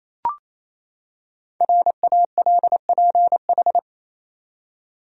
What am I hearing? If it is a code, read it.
Morse code, heard as a keyed tone, about 28 wpm, 700 Hz. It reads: RALPH
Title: USA Common Names - Max 5 Letters - 28wpm